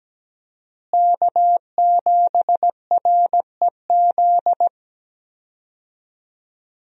Morse code, heard as a keyed tone, about 17 wpm, 700 Hz. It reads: K7REZ
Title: USA Call Signs - 5 Characters - Repeated 2x - 17wpm